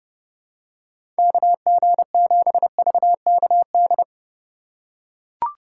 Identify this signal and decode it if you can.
Morse code, heard as a keyed tone, about 32 wpm, 700 Hz. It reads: KG7VKD
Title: USA Call Signs - 6 Characters - 3x Character Spacing - 30wpm